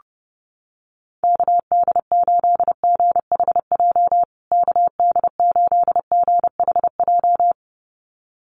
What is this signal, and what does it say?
Morse code, heard as a keyed tone, about 30 wpm, 700 Hz. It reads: KD8GHJ KD8GHJ
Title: USA Call Signs Repeated 2x 30wpm